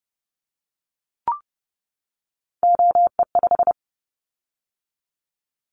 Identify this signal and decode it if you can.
Morse code, heard as a keyed tone, about 30 wpm, 700 Hz. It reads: OE5
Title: Callsign Prefixes for Top 50 DXCC - 30wpm